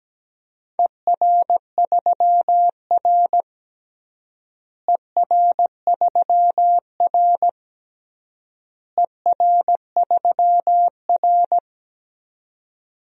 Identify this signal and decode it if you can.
Morse code, heard as a keyed tone, about 17 wpm, 700 Hz. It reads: ER3R ER3R ER3R
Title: DXCC-100 Call Signs - 4 Characters - Repeated 3x - 17wpm